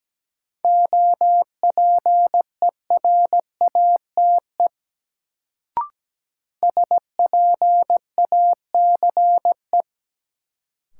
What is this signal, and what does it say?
Morse code, heard as a keyed tone, about 17 wpm, 700 Hz. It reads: OPERATE SPACE
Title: Sentences from Top 1500 Words - New Word Drill Alt - 17wpm